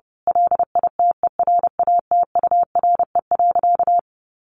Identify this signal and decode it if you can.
Morse code, heard as a keyed tone, about 30 wpm, 700 Hz. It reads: LITERATURE.
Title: Sentences from Top 900 Words 30wpm